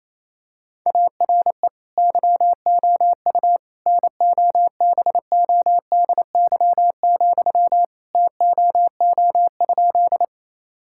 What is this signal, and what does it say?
Morse code, heard as a keyed tone, about 28 wpm, 700 Hz. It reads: ARE YOU NOBODY, TOO?
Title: Sentences from Top 1600 Words 28wpm